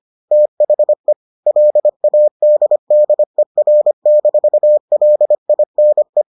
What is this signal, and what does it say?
Morse code, heard as a keyed tone, about 25 wpm, 600 Hz. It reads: THE LADDER-LINE